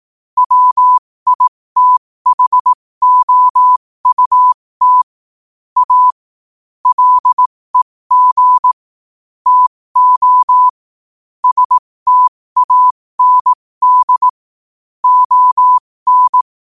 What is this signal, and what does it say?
Morse code, heard as a keyed tone, about 18 wpm, spaced as for 15 wpm, 1000 Hz. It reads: WITHOUT A LEG TO STAND ON
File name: qotdfiles / 15wpm